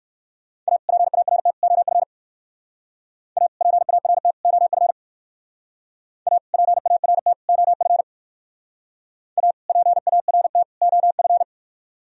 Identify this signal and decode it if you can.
Morse code, heard as a keyed tone, about 54 wpm, 700 Hz. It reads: A PART OF A PART OF A PART OF A PART OF
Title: Speed-Racing - Sets of 3 Words 45wpm